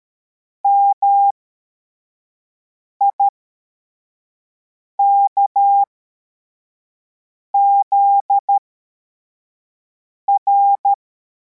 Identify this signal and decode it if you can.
Morse code, heard as a keyed tone, about 13 wpm, spaced as for 4 wpm, 800 Hz. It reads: MIKZR
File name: formation / cw / Text41